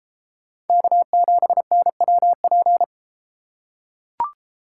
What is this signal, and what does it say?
Morse code, heard as a keyed tone, about 30 wpm, 700 Hz. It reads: K7NWP